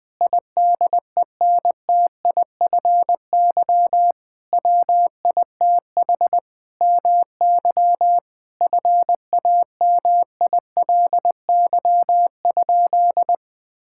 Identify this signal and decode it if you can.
Morse code, heard as a keyed tone, about 20 wpm, 700 Hz. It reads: IDENTIFY WITH MY FAMILY?
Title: Sentences from Top 1000 Words - Review Part 6 - 20wpm